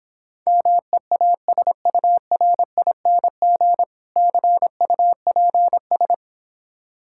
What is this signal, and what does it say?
Morse code, heard as a keyed tone, about 26 wpm, 700 Hz. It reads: MEASURING CUPS